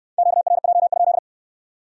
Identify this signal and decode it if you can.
Morse code, heard as a keyed tone, about 60 wpm, 700 Hz. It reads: QRJ?